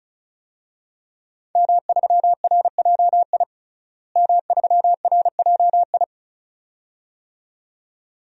Morse code, heard as a keyed tone, about 36 wpm, 700 Hz. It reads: M3RJI M3RJI
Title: DXCC-100 Call Signs - 5 Characters - Repeated 2x - 35wpm